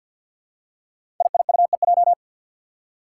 Morse code, heard as a keyed tone, about 47 wpm, 700 Hz. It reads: IIUEJ
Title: Five Challenging Letter-Number Combinations 50wpm